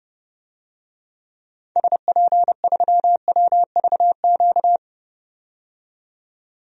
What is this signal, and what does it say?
Morse code, heard as a keyed tone, about 30 wpm, 700 Hz. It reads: SP3WVQ
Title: DXCC-100 Call Signs - 6 Characters - 30wpm